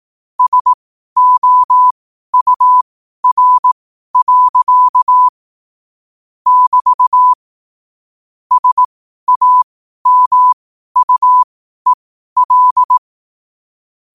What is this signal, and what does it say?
Morse code, heard as a keyed tone, about 18 wpm, spaced as for 12 wpm, 1000 Hz. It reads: SOUR. <BT> SAMUEL
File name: qotdfiles / 12wpm